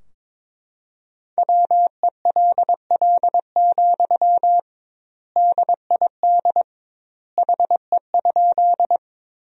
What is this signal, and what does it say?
Morse code, heard as a keyed tone, about 22 wpm, 700 Hz. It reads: WELL, DID HE?
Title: 2x Word-Spacing - Sentences from Top 200 Words - 22wpm